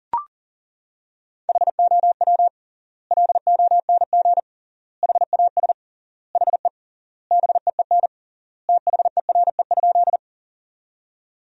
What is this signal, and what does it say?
Morse code, heard as a keyed tone, about 37 wpm, 700 Hz. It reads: HOW LONG HAS HE BEEN THERE?
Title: 3x Word-Spacing - Sentences from Top 100 Words - 40wpm